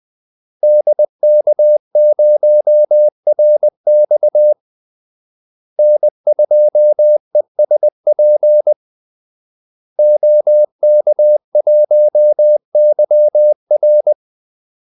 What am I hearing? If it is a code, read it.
Morse code, heard as a keyed tone, about 20 wpm, 600 Hz. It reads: DK0RX N2ESP OK1YR
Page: Kaldesignaler 20 wpm